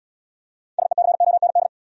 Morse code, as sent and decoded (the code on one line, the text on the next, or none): ... -.-. --- -. -..
SECOND